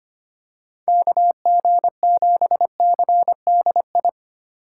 Morse code, heard as a keyed tone, about 25 wpm, 700 Hz. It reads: KG7CDI
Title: USA Call Signs - 6 Characters - 8x Character Spacing - 25wpm